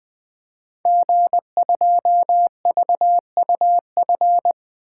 Morse code, heard as a keyed tone, about 20 wpm, 700 Hz. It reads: G2VUF
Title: Code Groups - 5 Letter:Numbers - 20wpm